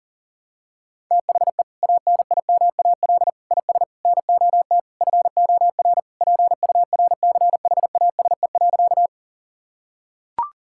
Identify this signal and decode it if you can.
Morse code, heard as a keyed tone, about 40 wpm, 700 Hz. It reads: THE ANIMAL IS NOT FOR PURCHASE.
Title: Sentences from Top 1000 Words - Review Part 5 - 40wpm